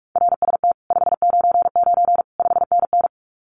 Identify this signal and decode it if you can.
Morse code, heard as a keyed tone, about 44 wpm, 700 Hz. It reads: RST 599 5NN
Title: Common QSO Phrases Encore 45wpm